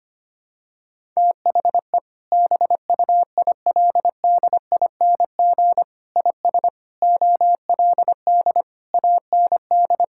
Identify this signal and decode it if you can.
Morse code, heard as a keyed tone, about 25 wpm, 700 Hz. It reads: THE BUILDING IS OLD AND
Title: Sentences from Top 1000 Words - Review Part 5 - 25wpm